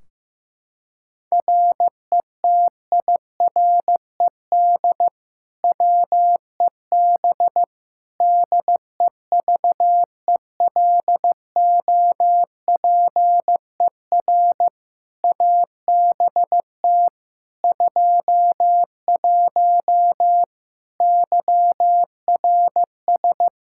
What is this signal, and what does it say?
Morse code, heard as a keyed tone, about 15 wpm, 700 Hz. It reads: RETIRED WEB DEVELOPER ABT 21 YRS
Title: Practice QSO - Exchange 3 - More Info - 15wpm